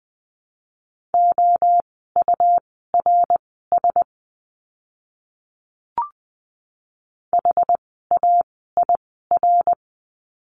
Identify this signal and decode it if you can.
Morse code, heard as a keyed tone, about 20 wpm, spaced as for 14 wpm, 700 Hz. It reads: OURS HAIR
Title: 101 Lesson 17f - Letter u - Words 20wpm